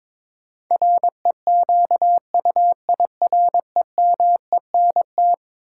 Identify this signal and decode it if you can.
Morse code, heard as a keyed tone, about 22 wpm, 700 Hz. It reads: REQUIREMENT
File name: practice / 22wpm / Eleven Letter Words - 22wpm